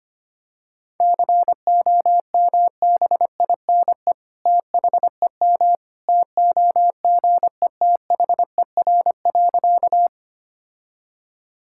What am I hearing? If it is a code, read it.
Morse code, heard as a keyed tone, about 25 wpm, 700 Hz. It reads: COMBINE THEM TOGETHER.
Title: Sentences from Top 1600 Words 25wpm